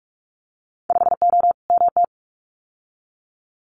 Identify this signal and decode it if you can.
Morse code, heard as a keyed tone, about 46 wpm, 700 Hz. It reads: 5O MT
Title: Winter Field Day Exchanges Encore 45wpm